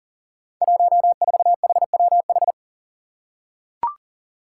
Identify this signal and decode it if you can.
Morse code, heard as a keyed tone, about 40 wpm, 700 Hz. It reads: 14HWH
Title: Five Challenging Letter-Number Combinations 40wpm